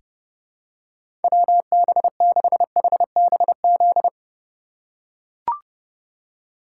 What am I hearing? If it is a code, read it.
Morse code, heard as a keyed tone, about 30 wpm, 700 Hz. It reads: WB6HBZ
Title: USA Call Signs - 6 Characters - 3x Character Spacing - 30wpm